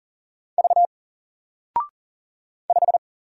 Morse code, heard as a keyed tone, about 40 wpm, 700 Hz. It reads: V 5
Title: Single Letter-Number Alt 40wpm